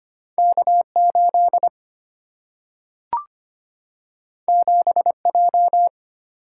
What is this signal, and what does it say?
Morse code, heard as a keyed tone, about 25 wpm, 700 Hz. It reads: K8 7J
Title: Single-Letter Single-Number 25wpm